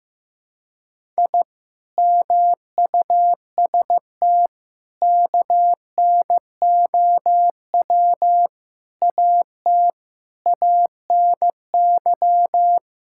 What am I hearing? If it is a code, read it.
Morse code, heard as a keyed tone, about 15 wpm, 700 Hz. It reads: I MUST KNOW AT ANY
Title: Sentences from Top 1000 Words - Review Part 5 - 15wpm